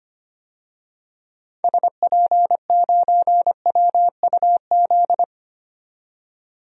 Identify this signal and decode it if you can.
Morse code, heard as a keyed tone, about 25 wpm, 700 Hz. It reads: SP9WUZ
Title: DXCC-100 Call Signs - 6 Characters - 25wpm